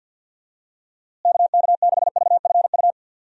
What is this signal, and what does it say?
Morse code, heard as a keyed tone, about 50 wpm, 700 Hz. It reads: KK6VFU
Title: USA Call Signs - 6 Characters - Repeated 3x - 50wpm